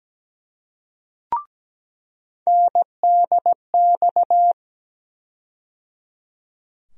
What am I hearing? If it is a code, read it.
Morse code, heard as a keyed tone, about 17 wpm, 700 Hz. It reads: NDX
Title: USA Call Sign Postfixes - 3 Letters 17wpm